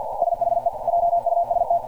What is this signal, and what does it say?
Morse code, heard as a keyed tone, about 48 wpm, 690 Hz. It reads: SHE JUST
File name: archived / CRSnet